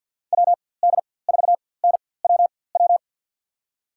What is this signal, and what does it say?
Morse code, heard as a keyed tone, about 51 wpm, spaced as for 23 wpm, 700 Hz. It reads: WD4NWW
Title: USA Call Signs - 6 Characters - 4x Character Spacing - 50wpm